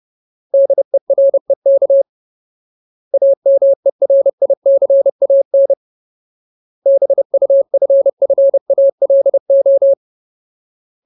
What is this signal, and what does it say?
Morse code, heard as a keyed tone, about 30 wpm, 550 Hz. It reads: DEREK AMERICAN BUFFALO
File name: Ultimate Word List / 30WPM Ultimate